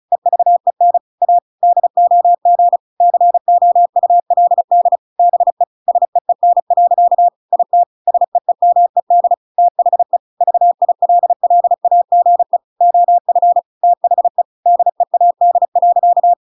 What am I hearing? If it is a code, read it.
Morse code, heard as a keyed tone, about 34 wpm, 700 Hz. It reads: EVEN A DOG COULD BE SEEN. IT SEEMED THE VILLAGE OF THE DEAD.